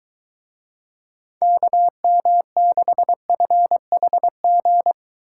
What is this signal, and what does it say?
Morse code, heard as a keyed tone, about 23 wpm, 700 Hz. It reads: KM6FHG